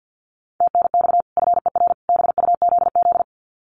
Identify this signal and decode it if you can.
Morse code, heard as a keyed tone, about 49 wpm, 700 Hz. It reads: TNX FER BUZZ
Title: Common QSO Phrases 50wpm